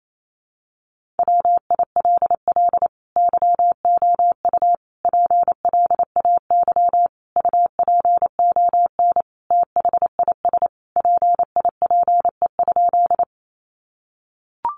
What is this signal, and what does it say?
Morse code, heard as a keyed tone, about 28 wpm, 700 Hz. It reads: WILL YOU PLAY UPON THIS PIPE?
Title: Sentences from Top 1500 Words 28wpm